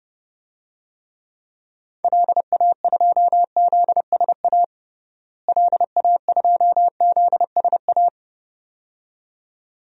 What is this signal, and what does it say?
Morse code, heard as a keyed tone, about 30 wpm, 700 Hz. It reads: LA2ZSA LA2ZSA
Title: DXCC-100 Call Signs - 6 Characters - Repeated 2x - 30wpm